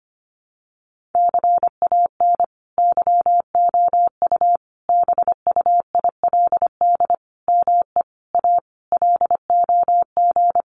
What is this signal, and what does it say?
Morse code, heard as a keyed tone, about 25 wpm, 700 Hz. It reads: CAN YOU BUILD ME A LOG
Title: Sentences from Top 1400 Words 25wpm